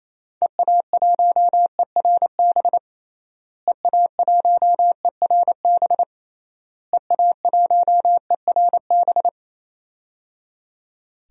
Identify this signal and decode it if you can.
Morse code, heard as a keyed tone, about 28 wpm, 700 Hz. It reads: EA1ERB EA1ERB EA1ERB
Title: DXCC-100 Call Signs - 6 Characters - Repeated 3x - 28wpm